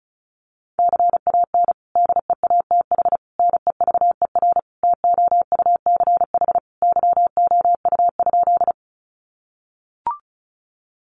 Morse code, heard as a keyed tone, about 34 wpm, 700 Hz. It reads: CAN DEATH NEVER TOUCH YOU?
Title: Sentences from Top 1200 Words 35wpm